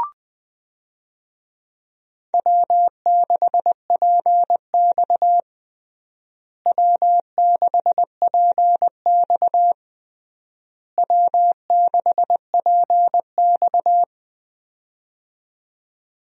Morse code, heard as a keyed tone, about 20 wpm, 700 Hz. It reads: W6PX W6PX W6PX
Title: USA Call Signs - 4 Characters - Repeated 3x - 20wpm